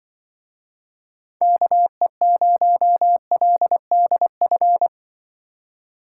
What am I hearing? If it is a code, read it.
Morse code, heard as a keyed tone, about 24 wpm, 700 Hz. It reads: KE0LDF
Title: Speed Racing - USA 2x3 Call Signs 20wpm